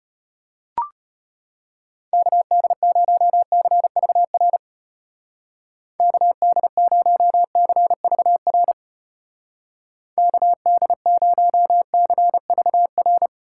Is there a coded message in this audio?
Yes: Morse code from 2.1 to 13.3 s, heard as a keyed tone, about 32 wpm, 700 Hz. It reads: KD0CVR KD0CVR KD0CVR